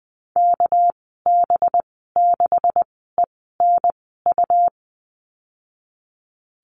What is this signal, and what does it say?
Morse code, heard as a keyed tone, about 20 wpm, spaced as for 15 wpm, 700 Hz. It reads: KB6ENU
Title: 238 Lesson 38g - Letter q - Call Signs 20wpm